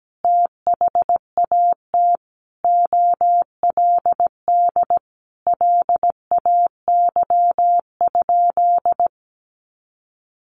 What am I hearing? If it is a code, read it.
Morse code, heard as a keyed tone, about 17 wpm, 700 Hz. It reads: THAT OLD LAY?
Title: Sentences from Top 400 Words 17wpm